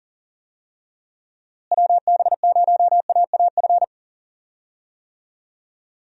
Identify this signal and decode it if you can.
Morse code, heard as a keyed tone, about 38 wpm, 700 Hz. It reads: WB0AAF
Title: USA Call Signs - 6 Characters - Repeated 3x - 40wpm